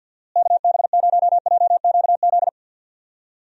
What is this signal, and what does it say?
Morse code, heard as a keyed tone, about 53 wpm, 700 Hz. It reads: KB0JQZ
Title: USA Call Signs - 6 Characters - Repeated 3x - 50wpm